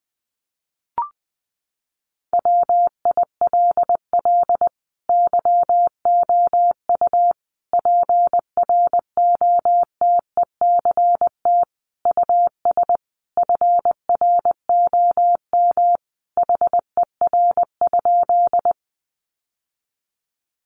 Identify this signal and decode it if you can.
Morse code, heard as a keyed tone, about 20 wpm, 700 Hz. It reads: WILL YOU PROTECT US FROM HER?
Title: Sentences from Top 900 Words 20wpm